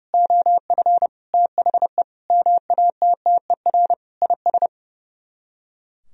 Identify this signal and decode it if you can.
Morse code, heard as a keyed tone, about 36 wpm, 700 Hz. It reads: OF THE MATTER IS
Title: Speed-Racing - Sets of 5 Words 30wpm